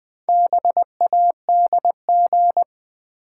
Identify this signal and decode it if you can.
Morse code, heard as a keyed tone, about 20 wpm, 700 Hz. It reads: BADG
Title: Build A Word - 1 Letter At A Time - 3 to 5-Letter Words - 20wpm